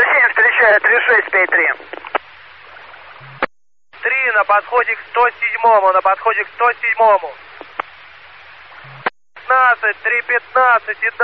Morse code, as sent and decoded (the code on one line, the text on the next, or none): none